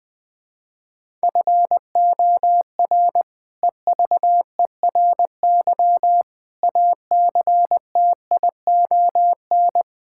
Morse code, heard as a keyed tone, about 20 wpm, 700 Hz. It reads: FOR EVERY ACTION
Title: Sentences with 8-Word Phrases - Spoken Beginning - 20wpm